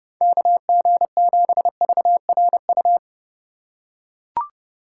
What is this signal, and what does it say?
Morse code, heard as a keyed tone, about 28 wpm, 700 Hz. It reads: KG7VRU